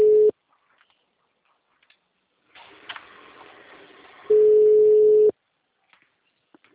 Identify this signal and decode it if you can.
Ringback tone (Europe, 425 Hz).
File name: out- -2023- - - .41